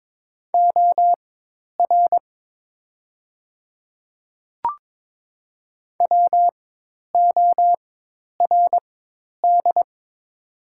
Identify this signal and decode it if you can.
Morse code, heard as a keyed tone, about 22 wpm, spaced as for 10 wpm, 690 Hz. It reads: OR WORD
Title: 4x Character Spacing - Top 300 Words - 22wpm